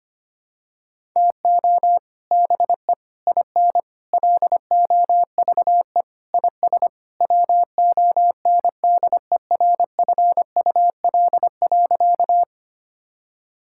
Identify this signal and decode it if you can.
Morse code, heard as a keyed tone, about 25 wpm, 700 Hz. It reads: TO BE IN LOVE IS WONDERFUL.
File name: practice / 25wpm / Sentences with 3-Word Phrases - 25wpm